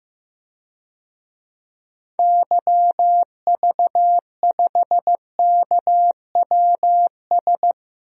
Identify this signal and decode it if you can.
Morse code, heard as a keyed tone, about 15 wpm, 700 Hz. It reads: YV5KWS
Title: DXCC-100 Call Signs - 6 Characters - 15wpm